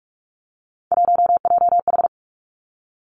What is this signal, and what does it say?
Morse code, heard as a keyed tone, about 46 wpm, 700 Hz. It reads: 1JH
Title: Three Challenging Letter-Number Combinations 45wpm